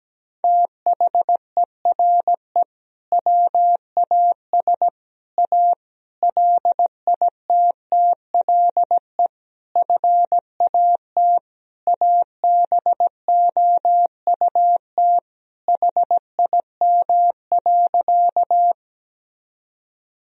Morse code, heard as a keyed tone, about 17 wpm, 700 Hz. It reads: THERE WAS A LITTLE FAT ABOUT HIM.
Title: Sentences from Top 1000 Words - Review Part 4 - 17wpm